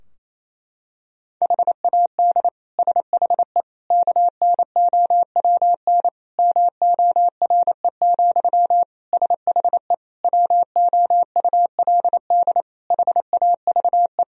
Morse code, heard as a keyed tone, about 28 wpm, 700 Hz. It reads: HAD SHE KNOWN MORE, SHE WOULD HAVE
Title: Sentences from Top 1000 Words - Review Part 3 - 28wpm